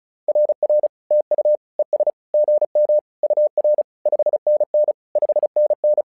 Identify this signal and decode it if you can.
Morse code, heard as a keyed tone, about 36 wpm, 600 Hz. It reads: RR TU ES GM UR 5NN 5NN
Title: SOTA QSO Practice 35wpm